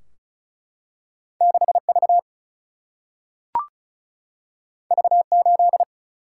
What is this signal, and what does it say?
Morse code, heard as a keyed tone, about 36 wpm, 700 Hz. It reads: 6V V8
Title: Single-Letter Single-Number 35wpm